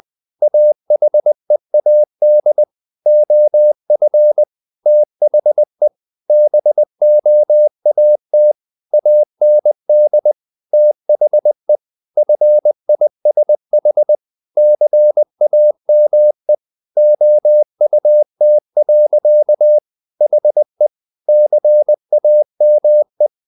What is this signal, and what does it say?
Morse code, heard as a keyed tone, about 20 wpm, 600 Hz. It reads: AHEAD OF THE BOAT AND THE FISH CAME OUT. HE CAME
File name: the old man and the sea / the old man and the sea part 1 20wpm